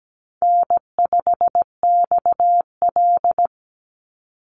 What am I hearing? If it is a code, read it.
Morse code, heard as a keyed tone, about 17 wpm, 700 Hz. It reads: N5XL
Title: NA SOTA Chaser Callsigns 17wpm